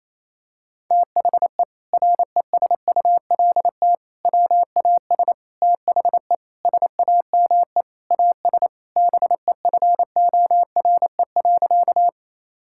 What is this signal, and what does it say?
Morse code, heard as a keyed tone, about 28 wpm, 700 Hz. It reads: THE RESULT WAS THE SAME AS BEFORE.
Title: Sentences from Top 1000 Words - Review Part 5 - 28wpm